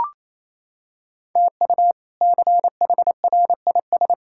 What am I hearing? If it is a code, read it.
Morse code, heard as a keyed tone, about 27 wpm, 700 Hz. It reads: TU CHRIS
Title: K1USN SST QSO Practice - 28wpm